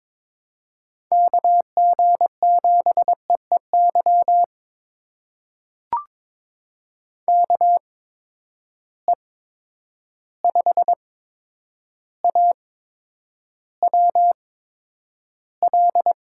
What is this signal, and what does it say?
Morse code, heard as a keyed tone, about 22 wpm, spaced as for 9 wpm, 700 Hz. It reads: KG7EEY KE5AWL
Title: USA Call Signs - 6 Characters - 8x Character Spacing - 22wpm